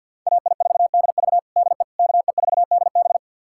Morse code, heard as a keyed tone, about 51 wpm, 700 Hz. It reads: AI4DV DE KE4DB
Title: USA Callsign DE Callsign - 5 Characters - 50wpm